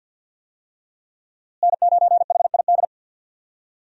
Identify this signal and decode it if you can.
Morse code, heard as a keyed tone, about 49 wpm, 700 Hz. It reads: N9HID
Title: USA Call Signs - 5 Characters - Repeated 2x - 50wpm